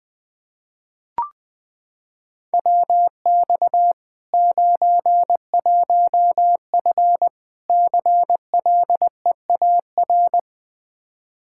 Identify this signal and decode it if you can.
Morse code, heard as a keyed tone, about 21 wpm, 700 Hz. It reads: WX 91F CLEAR
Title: Practice QSO - Exchange 2 - Info - 20wpm